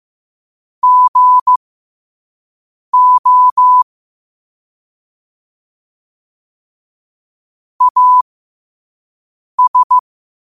Morse code, heard as a keyed tone, about 15 wpm, spaced as for 5 wpm, 1000 Hz. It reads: GO AS
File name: qotdfiles / 5wpm